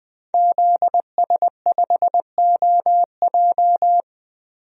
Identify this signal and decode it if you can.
Morse code, heard as a keyed tone, about 20 wpm, 700 Hz. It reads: ZS5OJ
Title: DXCC-100 Call Signs - 5 Characters - 20wpm